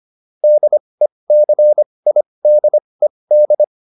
Morse code, heard as a keyed tone, about 25 wpm, spaced as for 20 wpm, 600 Hz. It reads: DECIDED